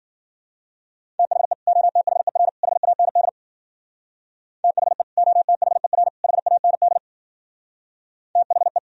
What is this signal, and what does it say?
Morse code, heard as a keyed tone, about 49 wpm, 700 Hz. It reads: THE OTHER HAND THE OTHER HAND THE
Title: Speed-Racing - Sets of 3 Words 40wpm